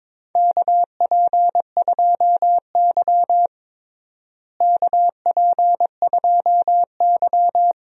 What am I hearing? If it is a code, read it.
Morse code, heard as a keyed tone, about 22 wpm, 700 Hz. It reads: KP2Y KP2Y
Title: DXCC-100 Call Signs - 4 Characters - Repeated 2x - 22wpm